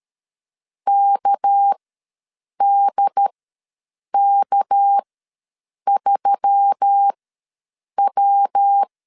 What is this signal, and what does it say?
Morse code, heard as a keyed tone, about 13 wpm, spaced as for 7 wpm, 800 Hz. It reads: KDK3W
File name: LEZIONI / 6 PKX0 / 6 PK ws 12wpm 3116